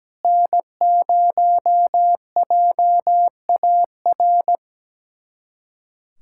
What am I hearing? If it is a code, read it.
Morse code, heard as a keyed tone, about 17 wpm, 700 Hz. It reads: N0JAR
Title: USA Callsign DE Callsign - 5 Characters - 17wpm